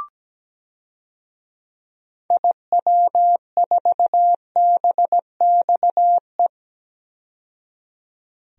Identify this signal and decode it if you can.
Morse code, heard as a keyed tone, about 17 wpm, 700 Hz. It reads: IW4BXE
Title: DXCC-100 Call Signs - 6 Characters - 17wpm